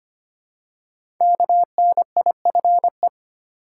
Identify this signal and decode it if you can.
Morse code, heard as a keyed tone, about 25 wpm, 690 Hz. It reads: KNIFE
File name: practice / 25wpm / Easy Words with Hints - 25wpm